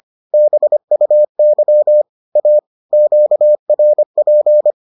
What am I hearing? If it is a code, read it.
Morse code, heard as a keyed tone, about 25 wpm, 600 Hz. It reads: BUY A QRP